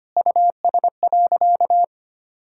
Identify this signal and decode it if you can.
Morse code, heard as a keyed tone, about 25 wpm, 700 Hz. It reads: US.
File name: practice / 25wpm / Sentences with 4-Word Phrases - 25wpm